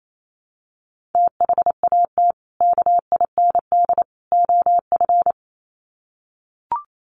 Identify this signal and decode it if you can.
Morse code, heard as a keyed tone, about 28 wpm, 700 Hz. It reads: THAT KIND OF
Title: Sets of 3 Words 28wpm